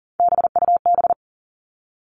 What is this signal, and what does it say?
Morse code, heard as a keyed tone, about 40 wpm, 700 Hz. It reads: BUB
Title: All Cut Numbers - 3-Digit 40wpm